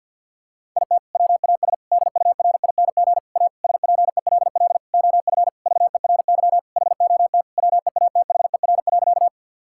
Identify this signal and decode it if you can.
Morse code, heard as a keyed tone, about 50 wpm, 700 Hz. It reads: IT WAS DURING A SPELL OF VERY HOT WEATHER.
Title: Sentences from Top 1000 Words - Review Part 5 - 50wpm